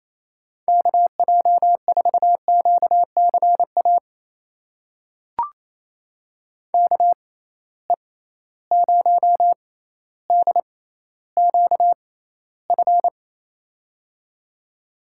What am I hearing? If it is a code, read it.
Morse code, heard as a keyed tone, about 28 wpm, spaced as for 14 wpm, 700 Hz. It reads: KJ4QCA KE0DQF
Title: USA Call Signs - 6 Characters - 6x Character Spacing - 28wpm